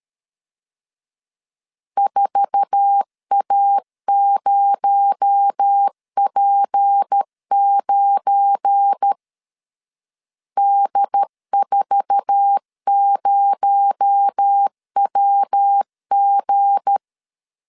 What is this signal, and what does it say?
Morse code, heard as a keyed tone, about 13 wpm, 800 Hz. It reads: 4A0P9 D40WG
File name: CW / DET / 1 9 1500C S B 12wpm 3819